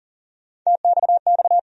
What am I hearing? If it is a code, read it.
Morse code, heard as a keyed tone, about 40 wpm, 700 Hz. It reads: TXX
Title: USA Call Sign Postfixes - 3 Letters 40wpm